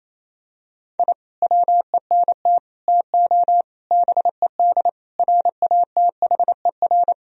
Morse code, heard as a keyed tone, about 28 wpm, 700 Hz. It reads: I WENT TO BED RATHER
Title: Sentences from Top 1000 Words - Review Part 4 - 28wpm